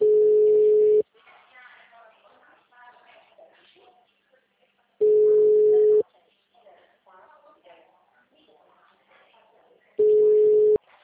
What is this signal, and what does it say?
Ringback tone (Europe, 425 Hz), with speech.
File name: out- -2013- - - .1385